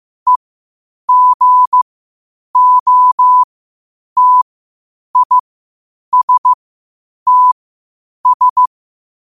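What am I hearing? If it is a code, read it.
Morse code, heard as a keyed tone, about 15 wpm, spaced as for 8 wpm, 1000 Hz. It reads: EGOTISTS
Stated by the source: Morse code quotes at 8 words per minute from QOTD at 8 WPM.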